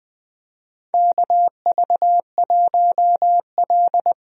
Morse code, heard as a keyed tone, about 21 wpm, 700 Hz. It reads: KV1L
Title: USA Call Signs - 4 Characters - Repeated 3x - 20wpm